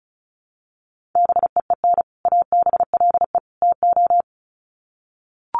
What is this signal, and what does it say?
Morse code, heard as a keyed tone, about 34 wpm, 700 Hz. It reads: BEEN ABLE TO
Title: Sets of 3 Words 35wpm